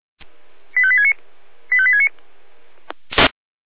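Five-tone selective call (ZVEI), heard as quick runs of five tones, tones of about 80 ms. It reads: 85869 85869
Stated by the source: ZVEI five-tone call 85220 received on SDR.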